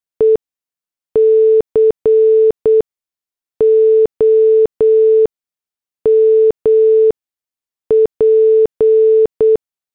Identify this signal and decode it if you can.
Morse code, heard as a keyed tone, about 8 wpm, spaced as for 6 wpm, 440 Hz. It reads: ECOMP